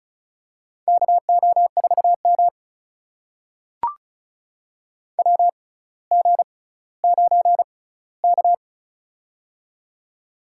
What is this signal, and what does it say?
Morse code, heard as a keyed tone, about 36 wpm, spaced as for 18 wpm, 700 Hz. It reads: KO4M WG9K
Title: USA Call Signs - 6x Character Spacing - 35wpm